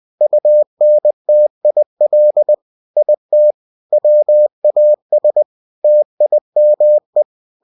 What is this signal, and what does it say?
Morse code, heard as a keyed tone, about 20 wpm, 600 Hz. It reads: UNTIL IT WAS TIME
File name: cw-books / winnie-the-pooh / winnie-20wpm-page-0033